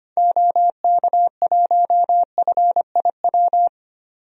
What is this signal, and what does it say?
Morse code, heard as a keyed tone, about 25 wpm, 700 Hz. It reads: OK1FIW
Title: DXCC-100 Call Signs - 6 Characters - 25wpm